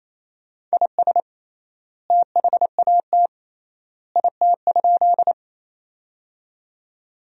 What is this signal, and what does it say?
Morse code, heard as a keyed tone, about 28 wpm, 700 Hz. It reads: IS THAT IT?
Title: 3x Word-Spacing - Sentences from Top 100 Words - 28wpm